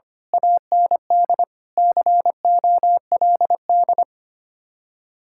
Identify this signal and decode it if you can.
Morse code, heard as a keyed tone, about 24 wpm, 700 Hz. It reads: AND COLD
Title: Speed-Racing Common QSO Phrases - 25wpm